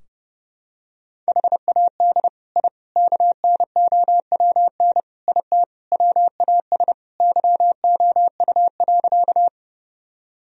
Morse code, heard as a keyed tone, about 30 wpm, 700 Hz. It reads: HAD I KNOWN IT WAS YOU.
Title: Sentences from Top 1000 Words - Review Part 3 - 30wpm